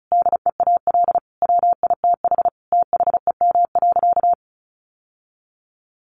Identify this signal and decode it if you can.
Morse code, heard as a keyed tone, about 34 wpm, 700 Hz. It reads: DEAL WITH THEM.
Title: Sentences from Top 800 Words 35wpm